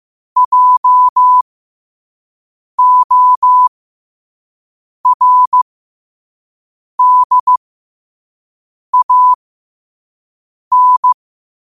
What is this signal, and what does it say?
Morse code, heard as a keyed tone, about 15 wpm, spaced as for 5 wpm, 1000 Hz. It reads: JORDAN
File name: qotdfiles / 5wpm